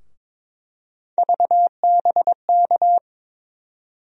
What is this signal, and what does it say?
Morse code, heard as a keyed tone, about 21 wpm, 700 Hz. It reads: VBK
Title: USA Call Sign Postfixes - 3 Letters 22wpm